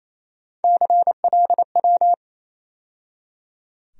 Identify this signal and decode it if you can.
Morse code, heard as a keyed tone, about 27 wpm, 700 Hz. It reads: CLW
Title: USA Call Sign Postfixes - 3 Letters 28wpm